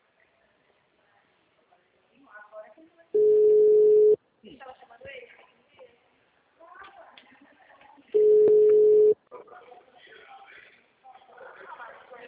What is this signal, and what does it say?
Ringback tone (Europe, 425 Hz), with speech.
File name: out- -2022- - - .11892